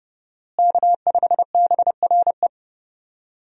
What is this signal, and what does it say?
Morse code, heard as a keyed tone, about 32 wpm, 700 Hz. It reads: K5BRE
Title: USA Call Signs - 5 Characters - Repeated 3x - 30wpm